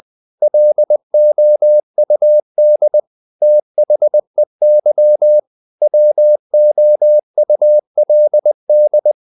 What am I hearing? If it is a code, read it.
Morse code, heard as a keyed tone, about 20 wpm, 600 Hz. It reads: LOUD THEY WOULD
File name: the old man and the sea / the old man and the sea part 1 20wpm